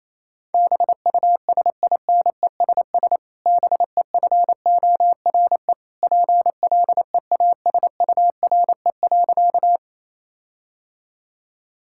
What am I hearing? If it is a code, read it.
Morse code, heard as a keyed tone, about 28 wpm, 700 Hz. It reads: BUSINESS BEFORE PLEASURE.
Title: Sentences from Top 1600 Words 28wpm